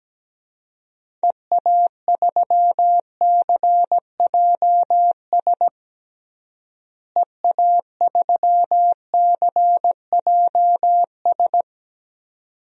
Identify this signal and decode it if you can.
Morse code, heard as a keyed tone, about 17 wpm, 700 Hz. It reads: EA3CJS EA3CJS
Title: DXCC-100 Call Signs - 6 Characters - Repeated 2x - 17wpm